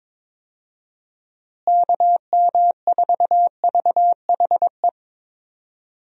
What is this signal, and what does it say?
Morse code, heard as a keyed tone, about 22 wpm, 700 Hz. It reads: KM4VHE
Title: USA Call Signs - 6 Characters - Repeated 2x - 22wpm